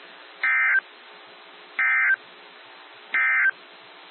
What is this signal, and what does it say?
Emergency Alert System (SAME) tones, heard as harsh buzzing bursts. It is heard from 0.3 to 3.6 s.
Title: NOAA Weather Radio - WXJ73